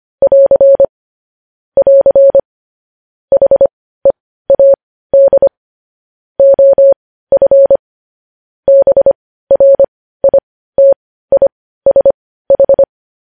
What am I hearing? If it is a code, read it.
Morse code, heard as a keyed tone, about 25 wpm, spaced as for 15 wpm, 550 Hz. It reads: <AR> <AR> HEAD OF BRITISH
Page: News Headlines in Morse Code at 15 WPM